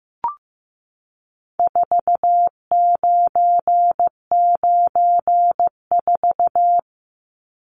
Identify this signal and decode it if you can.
Morse code, heard as a keyed tone, about 15 wpm, 700 Hz. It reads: 4994
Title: Numbers - 4-Digit 15wpm